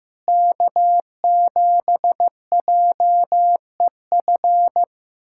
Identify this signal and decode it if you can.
Morse code, heard as a keyed tone, about 15 wpm, 700 Hz. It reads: K7JEF
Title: USA Callsign DE Callsign - 5 Characters - 15wpm